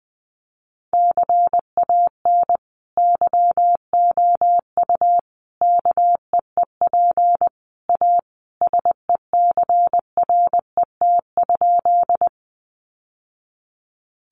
Morse code, heard as a keyed tone, about 20 wpm, 700 Hz. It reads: CAN YOU KEEP A SECRET?
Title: Sentences from Top 1200 Words 20wpm